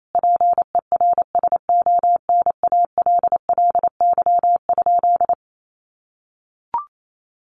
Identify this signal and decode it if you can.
Morse code, heard as a keyed tone, about 28 wpm, 700 Hz. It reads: PERSONALLY?
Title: Sentences from Top 1400 Words 28wpm